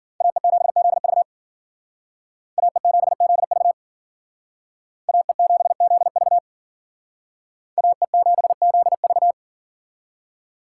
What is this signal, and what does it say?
Morse code, heard as a keyed tone, about 51 wpm, 700 Hz. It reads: AE7ZV AE7ZV AE7ZV AE7ZV